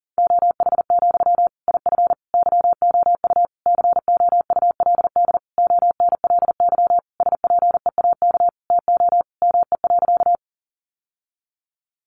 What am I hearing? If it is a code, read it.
Morse code, heard as a keyed tone, about 40 wpm, 700 Hz. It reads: OH, IF YOU COULD ONLY SPEAK TO ME.
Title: Sentences from Top 1100 Words 40wpm